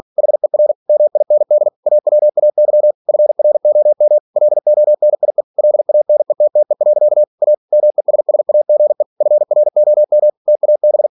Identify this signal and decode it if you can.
Morse code, heard as a keyed tone, about 46 wpm, 600 Hz. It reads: HER MIND AWAY FROM LONIE LANETTE. A MESSAGE FROM TAB
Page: Never 47wpm